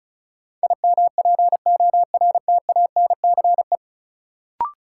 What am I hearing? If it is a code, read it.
Morse code, heard as a keyed tone, about 34 wpm, 700 Hz. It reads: IMPORTANCE
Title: Sentences from Top 1200 Words - New Word Drill Alt - 35wpm